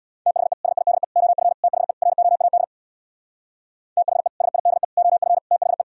Morse code, heard as a keyed tone, about 64 wpm, 700 Hz. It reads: THE SIDE OF THE ROAD THE SIDE OF THE